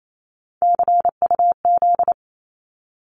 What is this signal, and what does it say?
Morse code, heard as a keyed tone, about 28 wpm, 700 Hz. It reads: CUZ
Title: Abbreviations 28wpm